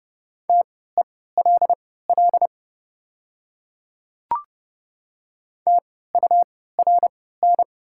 Morse code, heard as a keyed tone, about 30 wpm, spaced as for 17 wpm, 700 Hz. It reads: TELL TURN
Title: 3x Character Spacing - Top 200 Words - 30wpm